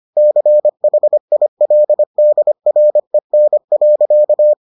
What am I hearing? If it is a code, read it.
Morse code, heard as a keyed tone, about 25 wpm, 600 Hz. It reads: CHILDREN.